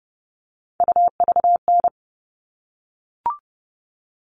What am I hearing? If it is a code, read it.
Morse code, heard as a keyed tone, about 30 wpm, 700 Hz. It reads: UVN
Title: All Cut Numbers - 3-Digit 30wpm